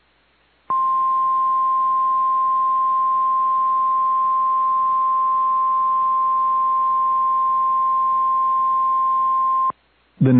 Emergency Alert System (SAME) tones (the 1050 Hz attention tone). It is heard from 0.7 to 9.7 s.